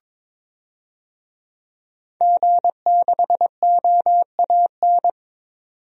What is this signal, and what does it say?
Morse code, heard as a keyed tone, about 22 wpm, 700 Hz. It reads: G6OAN
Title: DXCC-100 Call Signs - 5 Characters - 22wpm